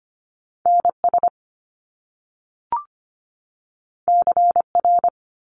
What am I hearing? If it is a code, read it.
Morse code, heard as a keyed tone, about 25 wpm, 700 Hz. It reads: NS CR
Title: Top Two Letter Combinations in Words 25wpm